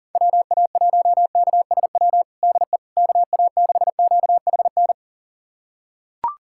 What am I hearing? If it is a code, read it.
Morse code, heard as a keyed tone, about 40 wpm, 700 Hz. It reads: WA1KSW DE KA6QHN
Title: USA Callsign DE Callsign - 6 Characters - 40wpm